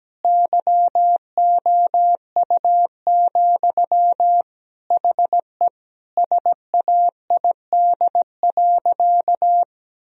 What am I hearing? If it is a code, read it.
Morse code, heard as a keyed tone, about 17 wpm, 700 Hz. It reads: YOU, HE SAID.
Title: Sentences from Top 1000 Words - Review Part 2 - 17wpm